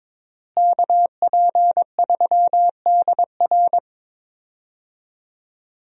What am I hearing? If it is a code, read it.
Morse code, heard as a keyed tone, about 21 wpm, 700 Hz. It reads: KP3DR
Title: USA Call Signs - 5 Characters - Repeated 2x - 22wpm